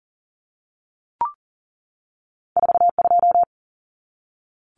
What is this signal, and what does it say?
Morse code, heard as a keyed tone, about 41 wpm, 700 Hz. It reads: 42
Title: Numbers - 2-Digit 40wpm